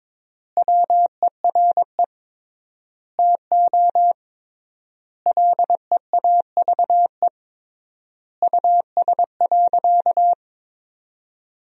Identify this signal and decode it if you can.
Morse code, heard as a keyed tone, about 22 wpm, 700 Hz. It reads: WERE TO LEAVE US.
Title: 3x Word-Spacing - Sentences from Top 400 Words - 22wpm